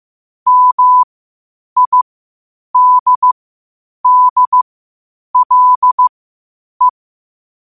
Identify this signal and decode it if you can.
Morse code, heard as a keyed tone, about 15 wpm, spaced as for 8 wpm, 1000 Hz. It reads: MIDDLE
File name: qotdfiles / 8wpm